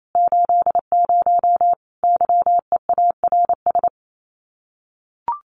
Morse code, heard as a keyed tone, about 28 wpm, 700 Hz. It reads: 80 YEARS
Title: Common QSO Phrases 28wpm